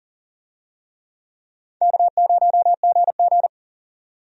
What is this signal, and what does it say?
Morse code, heard as a keyed tone, about 38 wpm, 700 Hz. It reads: K0GG
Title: USA Call Signs - 4 Characters - 40wpm